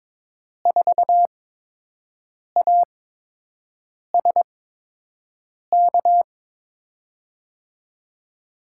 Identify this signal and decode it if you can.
Morse code, heard as a keyed tone, about 22 wpm, spaced as for 6 wpm, 700 Hz. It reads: 4ASK
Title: USA Call Signs - 5 Characters - 8x Character Spacing - 22wpm